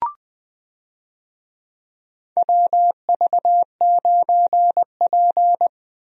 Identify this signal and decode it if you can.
Morse code, heard as a keyed tone, about 20 wpm, 700 Hz. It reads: WV9P
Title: USA Call Signs - 4 Characters - 20wpm